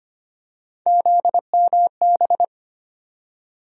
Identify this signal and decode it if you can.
Morse code, heard as a keyed tone, about 25 wpm, 700 Hz. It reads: ZMB
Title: USA Call Sign Postfixes - 3 Letters 25wpm